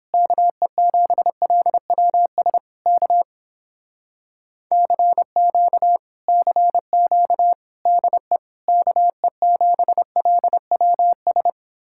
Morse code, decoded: KE7LWS K CQ CQ DE KE7LWS

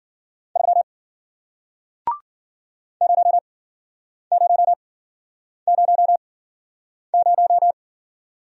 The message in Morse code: 3 0 0 0 0

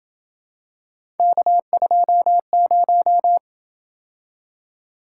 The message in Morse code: K20

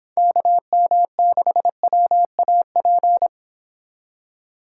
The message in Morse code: KM6WAP

